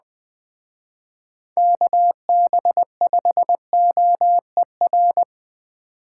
Morse code, heard as a keyed tone, about 20 wpm, 700 Hz. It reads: KB5OER
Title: Speed Racing - USA 2x3 Call Signs 17wpm